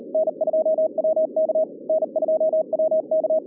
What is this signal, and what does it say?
Morse code, heard as a keyed tone, about 40 wpm, 660 Hz, with radio noise. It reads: N2WK N2WK
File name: 2011 bcc-challenge sonntag tape